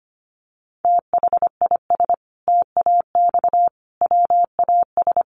Morse code, heard as a keyed tone, about 25 wpm, 700 Hz. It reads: THIS TAX WAS
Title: Sentences from Top 1500 Words 25wpm